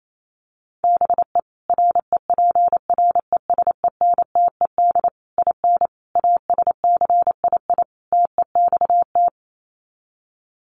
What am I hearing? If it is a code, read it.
Morse code, heard as a keyed tone, about 28 wpm, 700 Hz. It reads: BE REPRESENTED IN ASCII TEXT
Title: Sets of 5 Words 28wpm